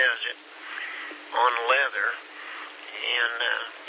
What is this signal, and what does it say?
Emergency Alert System (SAME) tones, heard as harsh buzzing bursts, with speech. It is heard from 0.6 to 3.7 s.